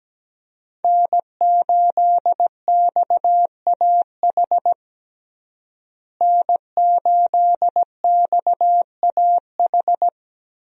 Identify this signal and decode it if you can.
Morse code, heard as a keyed tone, about 17 wpm, 700 Hz. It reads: N8XAH N8XAH
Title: USA Call Signs - 5 Characters - Repeated 3x - 17wpm